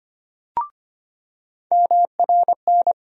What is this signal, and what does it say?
Morse code, heard as a keyed tone, about 25 wpm, 700 Hz. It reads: MRN
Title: USA Call Sign Postfixes - 3 Letters 25wpm